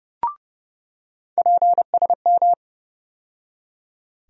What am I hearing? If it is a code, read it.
Morse code, heard as a keyed tone, about 30 wpm, 700 Hz. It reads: PSM
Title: USA Call Sign Postfixes - 3 Letters 30wpm